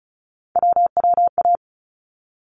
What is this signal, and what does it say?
Morse code, heard as a keyed tone, about 36 wpm, 700 Hz. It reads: WWA
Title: ARRL Sections 35wpm